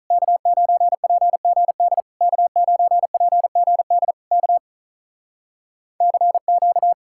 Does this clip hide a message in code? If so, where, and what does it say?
Yes: Morse code from 0.1 to 6.9 s, heard as a keyed tone, about 48 wpm, 700 Hz. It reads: K9PGD K9PGD K CQ